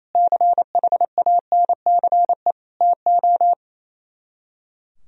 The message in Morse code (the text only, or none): CHANCE TO